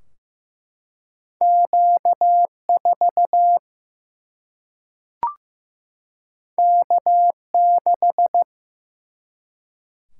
Morse code, heard as a keyed tone, about 15 wpm, 700 Hz. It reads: Q4 K6
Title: Single-Letter Single-Number 15wpm